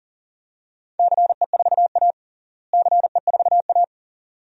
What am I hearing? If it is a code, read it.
Morse code, heard as a keyed tone, about 37 wpm, 700 Hz. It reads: CE4A CE4A
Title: DXCC-100 Call Signs - 4 Characters - Repeated 2x - 40wpm